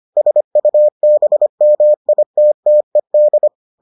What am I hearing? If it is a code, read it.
Morse code, heard as a keyed tone, about 25 wpm, 600 Hz. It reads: SUBMITTED